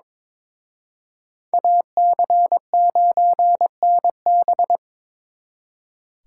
Morse code, heard as a keyed tone, about 26 wpm, 700 Hz. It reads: AC9NB
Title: Speed Racing - USA 2x2 Call Signs 22wpm